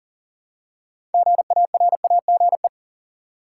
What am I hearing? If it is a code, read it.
Morse code, heard as a keyed tone, about 40 wpm, 700 Hz. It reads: GARAGE
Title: Six Letter Words - 40wpm